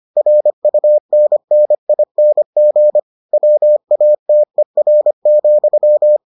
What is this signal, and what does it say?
Morse code, heard as a keyed tone, about 25 wpm, 600 Hz. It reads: RUNNING WATER,